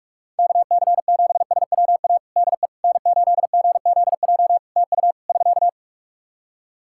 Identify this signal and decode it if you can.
Morse code, heard as a keyed tone, about 46 wpm, 700 Hz. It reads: KC7SWA DE N8GZJ TU <SK>
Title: Practice QSO - Exchange 4 - Ending - 45wpm